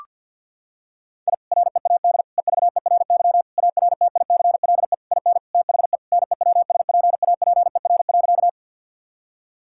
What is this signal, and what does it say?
Morse code, heard as a keyed tone, about 48 wpm, 700 Hz. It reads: I READ EVERY ARTICLE IN THE NEWSPAPER.
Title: Sentences from Top 1000 Words - Review Part 5 - 50wpm